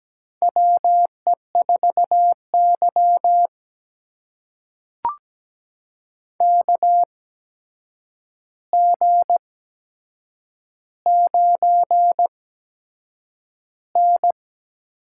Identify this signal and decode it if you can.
Morse code, heard as a keyed tone, about 17 wpm, spaced as for 7 wpm, 700 Hz. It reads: WE4Y KG9N
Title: USA Call Signs - 4 Characters - 8x Character Spacing - 17wpm